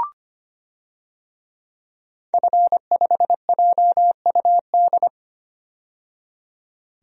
Morse code, heard as a keyed tone, about 25 wpm, 700 Hz. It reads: F5JUD
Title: DXCC-100 Call Signs - 5 Characters - 25wpm